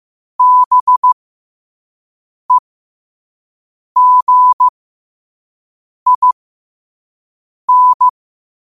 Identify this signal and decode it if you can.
Morse code, heard as a keyed tone, about 15 wpm, spaced as for 5 wpm, 1000 Hz. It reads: BEGIN
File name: 5wpm